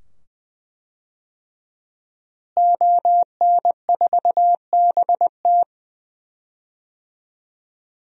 Morse code, heard as a keyed tone, about 20 wpm, 700 Hz. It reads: ON4BT
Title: DXCC-100 Call Signs - 5 Characters - Repeated 2x - 20wpm